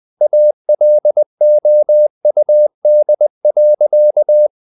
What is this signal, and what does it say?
Morse code, heard as a keyed tone, about 20 wpm, 600 Hz. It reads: ALOUD.
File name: the old man and the sea part 2 20wpm